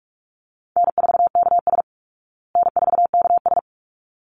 Morse code, heard as a keyed tone, about 44 wpm, 700 Hz. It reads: N4KS N4KS
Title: Easy Call Signs Repeated 2x - 45wpm